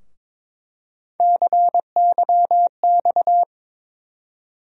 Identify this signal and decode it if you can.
Morse code, heard as a keyed tone, about 22 wpm, 700 Hz. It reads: CYX
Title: USA Call Sign Postfixes - 3 Letters 22wpm